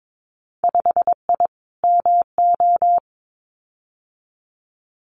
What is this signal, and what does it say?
Morse code, heard as a keyed tone, about 22 wpm, 700 Hz. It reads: 5I MO
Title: Winter Field Day Exchanges Encore 22wpm